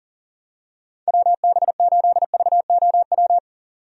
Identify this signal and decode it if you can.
Morse code, heard as a keyed tone, about 37 wpm, 700 Hz. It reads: WB8VOW
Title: USA Call Signs - 6 Characters - 40wpm